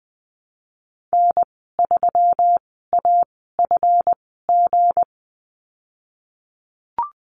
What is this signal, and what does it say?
Morse code, heard as a keyed tone, about 20 wpm, spaced as for 15 wpm, 700 Hz. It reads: N3AFG
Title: 179 Lesson 29g - Slash - Call Signs 20wpm